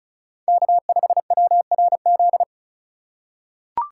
Morse code, heard as a keyed tone, about 34 wpm, 690 Hz. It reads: K5WRZ
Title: USA Call Signs - 5 Characters - 3x Character Spacing - 35wpm